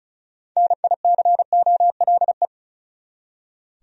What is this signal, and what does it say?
Morse code, heard as a keyed tone, about 34 wpm, 700 Hz. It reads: NICOLE
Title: USA Common Names - 35wpm